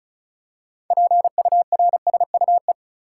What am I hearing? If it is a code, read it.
Morse code, heard as a keyed tone, about 36 wpm, 700 Hz. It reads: PURSUE
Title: Six Letter Words - 35wpm